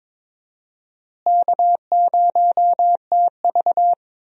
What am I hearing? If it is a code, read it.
Morse code, heard as a keyed tone, about 21 wpm, 700 Hz. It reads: K0TV
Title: USA Call Signs - 4 Characters - Repeated 3x - 22wpm